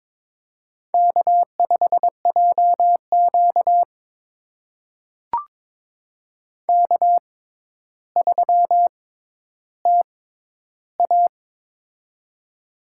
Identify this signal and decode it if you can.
Morse code, heard as a keyed tone, about 23 wpm, spaced as for 11 wpm, 700 Hz. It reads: K5JQ K3TA
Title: USA Call Signs - 4 Characters - 6x Character Spacing - 22wpm